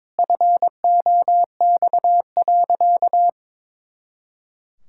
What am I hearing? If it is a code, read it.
Morse code, heard as a keyed tone, about 22 wpm, 700 Hz. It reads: FOX.
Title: Pangram Challenge v2 - 22wpm